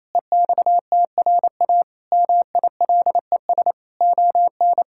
Morse code, heard as a keyed tone, about 27 wpm, 700 Hz. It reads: EXTRA MILES ON
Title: Pangram Challenge v2 - 28wpm